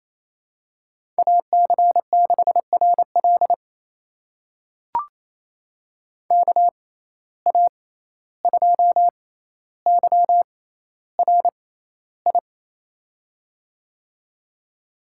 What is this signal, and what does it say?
Morse code, heard as a keyed tone, about 27 wpm, spaced as for 14 wpm, 700 Hz. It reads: AC6RL KA2YRI
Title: USA Call Signs - 6x Character Spacing - 28wpm